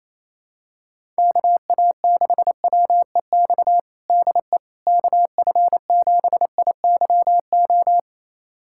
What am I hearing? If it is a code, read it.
Morse code, heard as a keyed tone, about 28 wpm, 700 Hz. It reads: KA6WEX DE KF7IYO
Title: USA Callsign DE Callsign - 6 Characters - 28wpm